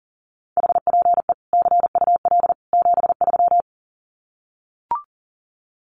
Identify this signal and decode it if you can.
Morse code, heard as a keyed tone, about 37 wpm, 700 Hz. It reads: HPE CUL 73
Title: Common QSO Phrases 40wpm